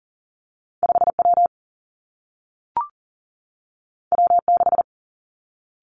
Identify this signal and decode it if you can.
Morse code, heard as a keyed tone, about 42 wpm, 700 Hz. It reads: 5 W W 6
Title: Two Challenging Letter-Number Combinations 40wpm